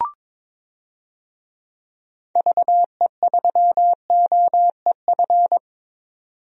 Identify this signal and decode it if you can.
Morse code, heard as a keyed tone, about 22 wpm, 700 Hz. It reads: VE3OEF
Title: DXCC-100 Call Signs - 6 Characters - 22wpm